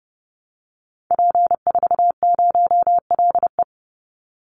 Morse code, heard as a keyed tone, about 30 wpm, 700 Hz. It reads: P40LE
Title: Call Signs - 30wpm